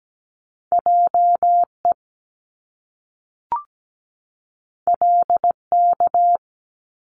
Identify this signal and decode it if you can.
Morse code, heard as a keyed tone, about 17 wpm, 700 Hz. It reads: JE LK
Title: Top Two Letter Combinations in Words 17wpm